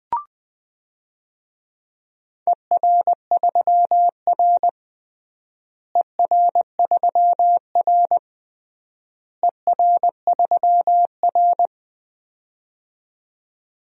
Morse code, heard as a keyed tone, about 21 wpm, 700 Hz. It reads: ER3R ER3R ER3R
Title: DXCC-100 Call Signs - 4 Characters - Repeated 3x - 20wpm